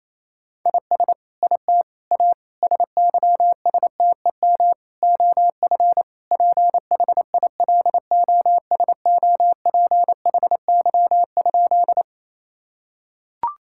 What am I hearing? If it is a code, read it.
Morse code, heard as a keyed tone, about 28 wpm, 700 Hz. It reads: IS IT A SYSTEM OF PHILOSOPHY?
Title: Sentences from Top 1000 Words - Review Part 6 - 28wpm